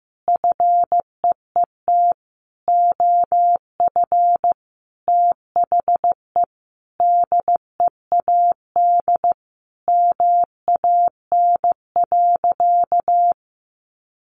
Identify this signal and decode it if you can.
Morse code, heard as a keyed tone, about 14 wpm, 700 Hz. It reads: FEET OF THE DEAD MAN.
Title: Sentences from Top 1500 Words 15wpm